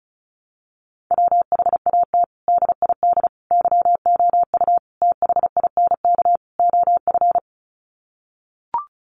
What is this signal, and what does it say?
Morse code, heard as a keyed tone, about 34 wpm, 700 Hz. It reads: WHAT DID YOU THINK OF
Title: Sets of 5 Words Encore2 35wpm